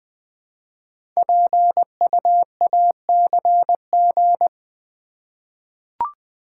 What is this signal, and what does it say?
Morse code, heard as a keyed tone, about 20 wpm, 700 Hz. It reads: PUACG
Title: Code Groups - 5 Letters - 20wpm